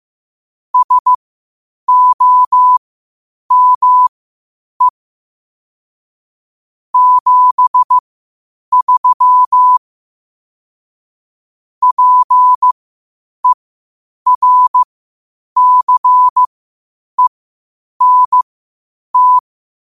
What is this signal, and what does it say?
Morse code, heard as a keyed tone, about 15 wpm, spaced as for 8 wpm, 1000 Hz. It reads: SOME 73 PERCENT
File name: qotdfiles / 8wpm